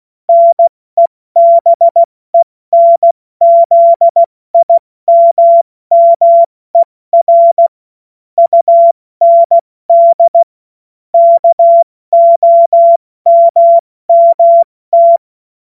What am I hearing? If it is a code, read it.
Morse code, heard as a keyed tone, about 16 wpm, 680 Hz. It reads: NEBENZIMMER UND KOMMT
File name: Morsekurs / Witze / Witz 21. Dezember 2021, 70 F80